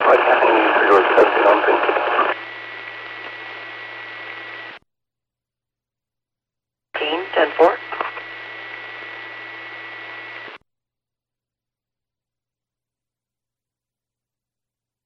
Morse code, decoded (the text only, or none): none